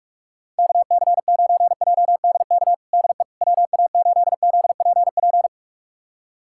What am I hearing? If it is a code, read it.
Morse code, heard as a keyed tone, about 46 wpm, 700 Hz. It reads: KC9JDK DE WA8ZPP
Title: USA Callsign DE Callsign - 6 Characters - 45wpm